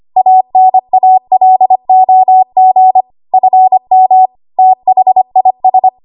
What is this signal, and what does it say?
Morse code, heard as a keyed tone, about 25 wpm, 750 Hz. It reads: ANALOG FM THIS
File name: Archive / 25 WPM / 25WPM